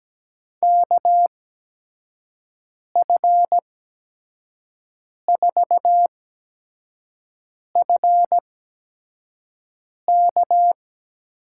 Morse code, heard as a keyed tone, about 17 wpm, spaced as for 5 wpm, 700 Hz. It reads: KF4FK
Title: USA Call Signs - 8x Character Spacing - 17wpm